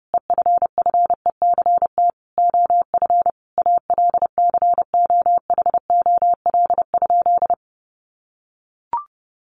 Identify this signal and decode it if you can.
Morse code, heard as a keyed tone, about 28 wpm, 700 Hz. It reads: EFFECT OF ALCOHOL?
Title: Sentences from Top 1400 Words 30wpm